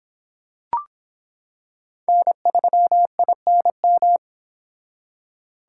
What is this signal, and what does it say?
Morse code, heard as a keyed tone, about 26 wpm, 700 Hz. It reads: N3INM